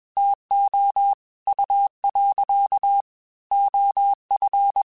Morse code, heard as a keyed tone, about 21 wpm, 800 Hz. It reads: TO U. OF
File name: QSOs / QSOs-mp3s / WA7QOX 1@20